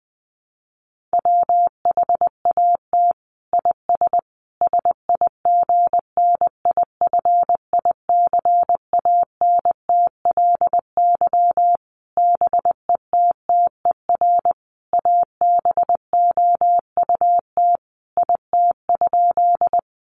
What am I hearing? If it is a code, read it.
Morse code, heard as a keyed tone, about 20 wpm, 700 Hz. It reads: WHAT IS SIGNIFICANTLY BETTER ABOUT IT?
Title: Sentences from Top 1500 Words 20wpm